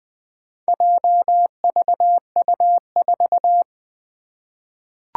Morse code, heard as a keyed tone, about 20 wpm, 700 Hz. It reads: JVU4
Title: Four Challenging Letter-Number Combinations 20wpm